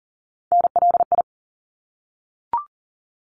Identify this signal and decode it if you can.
Morse code, heard as a keyed tone, about 40 wpm, 700 Hz. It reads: NLI
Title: ARRL Sections 40wpm